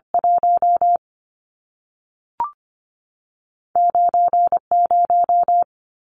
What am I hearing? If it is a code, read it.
Morse code, heard as a keyed tone, about 25 wpm, spaced as for 3 wpm, 700 Hz. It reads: 19 0
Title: Numbers - 2-Digit 25wpm